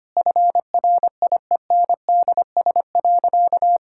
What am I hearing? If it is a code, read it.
Morse code, heard as a keyed tone, about 25 wpm, 700 Hz. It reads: FRIENDS.
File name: practice / 25wpm / Sentences with 4-Word Phrases - 25wpm